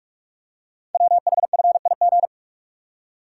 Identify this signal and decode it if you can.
Morse code, heard as a keyed tone, about 44 wpm, 700 Hz. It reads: WHFIG
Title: Code Groups - 5 Letters - 45wpm